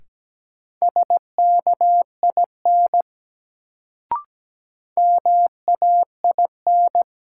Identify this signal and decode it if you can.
Morse code, heard as a keyed tone, about 17 wpm, 700 Hz. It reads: SKIN MAIN
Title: Sentences from Top 1300 Words - New Word Drill Alt - 17wpm